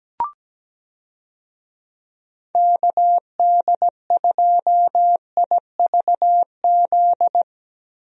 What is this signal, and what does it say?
Morse code, heard as a keyed tone, about 17 wpm, 700 Hz. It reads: KD2IVZ
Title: USA Call Signs - 6 Characters - Repeated 2x - 17wpm